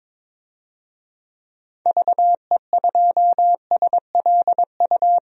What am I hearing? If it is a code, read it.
Morse code, heard as a keyed tone, about 22 wpm, 700 Hz. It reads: VE2SLU
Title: DXCC-100 Call Signs - 6 Characters - 22wpm